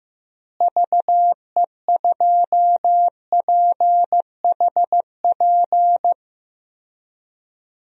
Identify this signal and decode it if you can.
Morse code, heard as a keyed tone, about 16 wpm, 700 Hz. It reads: VE2PHP
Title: DXCC-100 Call Signs - 6 Characters - Repeated 2x - 15wpm